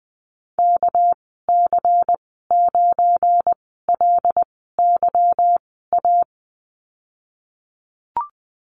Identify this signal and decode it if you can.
Morse code, heard as a keyed tone, about 20 wpm, spaced as for 15 wpm, 700 Hz. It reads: KC9LYA
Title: 200 Lesson 32g - Letter k - Call Signs 20wpm